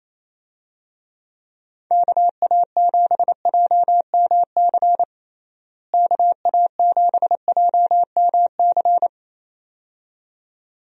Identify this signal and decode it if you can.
Morse code, heard as a keyed tone, about 27 wpm, 700 Hz. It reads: KA7JMC KA7JMC
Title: USA Call Signs - 6 Characters - Repeated 2x - 28wpm